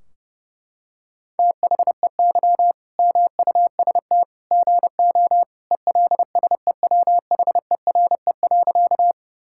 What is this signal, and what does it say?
Morse code, heard as a keyed tone, about 30 wpm, 700 Hz. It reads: THEY MUST GO ELSEWHERE.
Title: Sentences from Top 1600 Words 30wpm